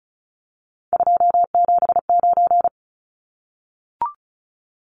Morse code, heard as a keyed tone, about 34 wpm, 700 Hz. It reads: 279
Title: Numbers - 3-Digit 35wpm